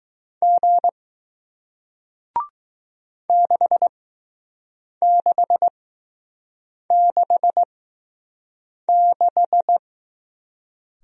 Morse code, heard as a keyed tone, about 20 wpm, spaced as for 6 wpm, 700 Hz. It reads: G 6666